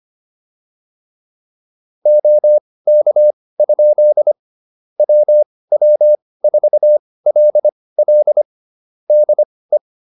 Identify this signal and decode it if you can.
Morse code, heard as a keyed tone, about 25 wpm, spaced as for 18 wpm, 600 Hz. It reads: OK? WW4LL DE